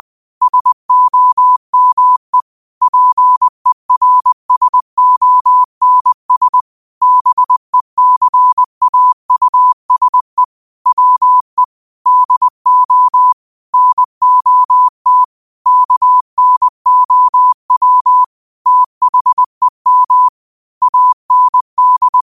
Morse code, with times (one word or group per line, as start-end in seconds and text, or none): 0.4-2.4 SOME
2.8-6.6 PERSONS
7.0-10.4 BECAUSE
10.9-11.6 WE
12.1-13.3 DO
13.7-15.2 NOT
15.7-18.2 KNOW
18.7-20.3 THEM
20.8-22.2 AND